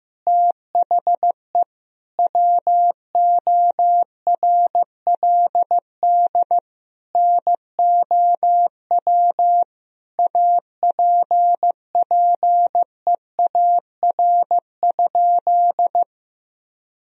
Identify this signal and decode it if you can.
Morse code, heard as a keyed tone, about 15 wpm, 700 Hz. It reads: THE WORLD NOW APPEAR?
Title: Sentences from Top 1000 Words - Review Part 2 - 15wpm